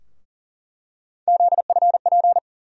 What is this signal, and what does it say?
Morse code, heard as a keyed tone, about 37 wpm, 700 Hz. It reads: ZFP
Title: USA Call Sign Postfixes - 3 Letters 40wpm